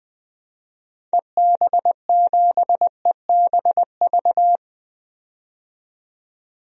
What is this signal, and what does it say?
Morse code, heard as a keyed tone, about 20 wpm, 700 Hz. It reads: EB7EBV
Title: DXCC-100 Call Signs - 6 Characters - 20wpm